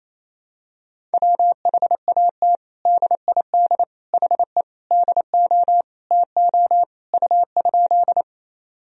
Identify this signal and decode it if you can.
Morse code, heard as a keyed tone, about 28 wpm, 700 Hz. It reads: WHAT DID HE DO TO U?
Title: Sentences from Top 100 Words and Abbreviations 28wpm